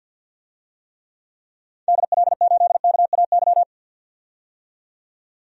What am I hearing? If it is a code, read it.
Morse code, heard as a keyed tone, about 48 wpm, 700 Hz. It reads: DL8KAY
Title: DXCC-100 Call Signs - 6 Characters - 50wpm